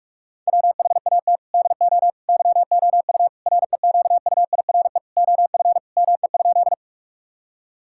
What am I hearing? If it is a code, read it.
Morse code, heard as a keyed tone, about 46 wpm, 700 Hz. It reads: WHAT DO YOU REQUIRE OF ME?
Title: Sentences from Top 1000 Words - Review Part 6 - 45wpm